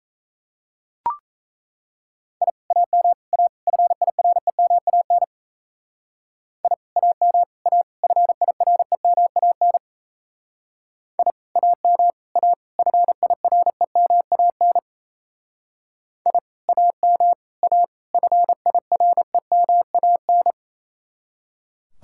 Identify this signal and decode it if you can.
Morse code, heard as a keyed tone, about 36 wpm, 700 Hz. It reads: I AM A FIREMAN I AM A FIREMAN I AM A FIREMAN I AM A FIREMAN